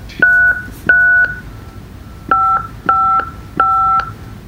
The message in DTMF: 555